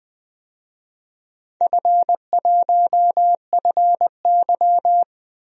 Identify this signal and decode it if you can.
Morse code, heard as a keyed tone, about 20 wpm, 700 Hz. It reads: F1FY
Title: DXCC-100 Call Signs - 4 Characters - Repeated 3x - 20wpm